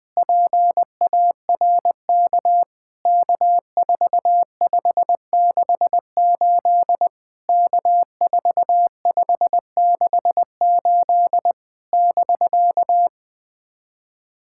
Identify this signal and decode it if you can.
Morse code, heard as a keyed tone, about 20 wpm, 700 Hz. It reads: PARK K4568 K4568 <BK>
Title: POTA QSO Practice - 20wpm